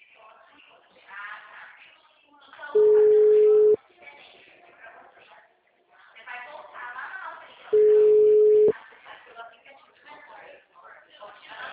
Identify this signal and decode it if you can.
Ringback tone (Europe, 425 Hz), with speech.